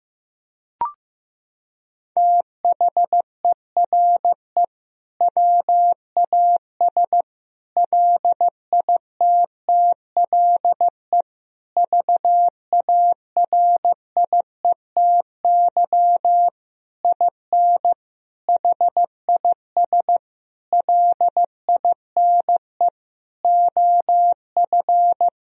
Morse code, heard as a keyed tone, about 15 wpm, 700 Hz. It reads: THERE WAS LITTLE VARIETY IN HIS LINE OF
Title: Sentences from Top 1000 Words - Review Part 4 - 15wpm